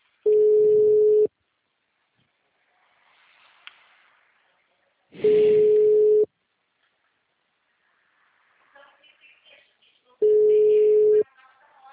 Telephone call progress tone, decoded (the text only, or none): ringback tone